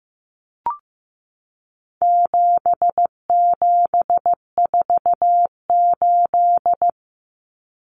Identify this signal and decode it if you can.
Morse code, heard as a keyed tone, about 15 wpm, 700 Hz. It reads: 7748
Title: Common Cut Numbers - 4-Digit 15wpm